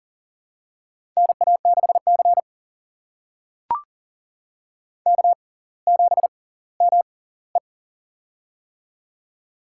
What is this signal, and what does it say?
Morse code, heard as a keyed tone, about 38 wpm, spaced as for 20 wpm, 700 Hz. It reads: NA6C K7ME
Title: USA Call Signs - 4 Characters - 6x Character Spacing - 40wpm